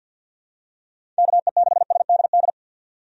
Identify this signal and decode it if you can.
Morse code, heard as a keyed tone, about 47 wpm, 700 Hz. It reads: KE6SDD
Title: USA Call Signs - 6 Characters - 50wpm